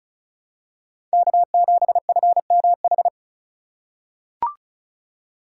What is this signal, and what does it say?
Morse code, heard as a keyed tone, about 34 wpm, 700 Hz. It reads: K7FMH
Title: USA Call Signs - 5 Characters - 3x Character Spacing - 35wpm